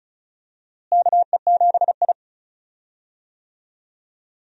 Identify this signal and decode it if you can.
Morse code, heard as a keyed tone, about 36 wpm, 700 Hz. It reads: KE7I
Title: USA Call Signs - 4 Characters - Repeated 2x - 35wpm